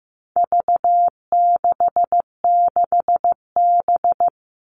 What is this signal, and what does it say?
Morse code, heard as a keyed tone, about 15 wpm, 700 Hz. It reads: V66B
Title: All Cut Numbers - 4-Digit 15wpm